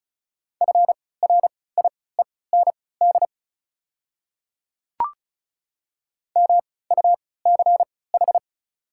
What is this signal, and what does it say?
Morse code, heard as a keyed tone, about 36 wpm, spaced as for 20 wpm, 690 Hz. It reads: FRIEND MUCH
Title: 3x Character Spacing - Top 500 Words - 35wpm